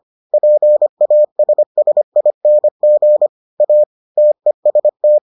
Morse code, heard as a keyed tone, about 25 wpm, 600 Hz. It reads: PASSING A TEST